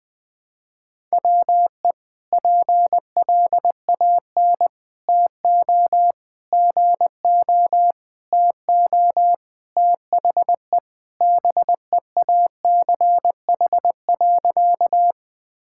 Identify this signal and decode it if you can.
Morse code, heard as a keyed tone, about 20 wpm, 700 Hz. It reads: WE PLAN TO GO TO THE BEACH.
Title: Sentences with 4-Word Phrases - 20wpm